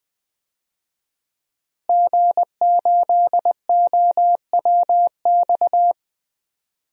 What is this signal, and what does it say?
Morse code, heard as a keyed tone, about 20 wpm, 700 Hz. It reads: G8OWX
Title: DXCC-100 Call Signs - 5 Characters - Repeated 3x - 20wpm